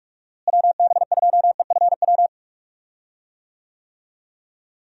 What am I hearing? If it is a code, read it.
Morse code, heard as a keyed tone, about 46 wpm, 700 Hz. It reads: WB2EFW
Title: USA Call Signs - 6 Characters - Repeated 3x - 45wpm